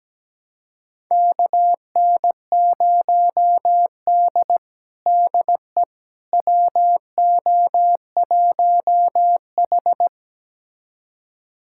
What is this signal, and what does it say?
Morse code, heard as a keyed tone, about 17 wpm, 700 Hz. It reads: KN0D DE WO1H
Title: USA Callsign DE Callsign - 4 Characters - 17wpm